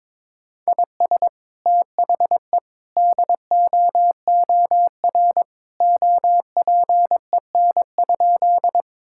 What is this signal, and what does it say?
Morse code, heard as a keyed tone, about 22 wpm, 700 Hz. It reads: IS THE DOOR OPEN?
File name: practice / 22wpm / Sentences with 2-Word Phrases - 22wpm